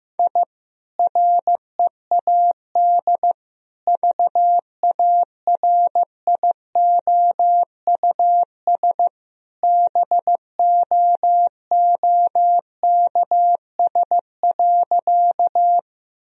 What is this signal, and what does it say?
Morse code, heard as a keyed tone, about 16 wpm, 700 Hz. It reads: I READ VARIOUS BOOKS.
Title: Sentences from Top 1000 Words - Review Part 4 - 15wpm